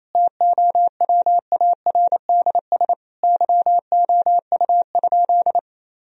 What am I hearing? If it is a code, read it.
Morse code, heard as a keyed tone, about 28 wpm, 700 Hz. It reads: TOWARDS YOU?
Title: Sentences from Top 1000 Words - Review Part 2 - 28wpm